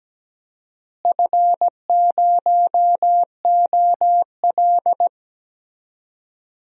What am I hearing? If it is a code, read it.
Morse code, heard as a keyed tone, about 17 wpm, 700 Hz. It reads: F0OL
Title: DXCC-100 Call Signs - 4 Characters - Repeated 3x - 17wpm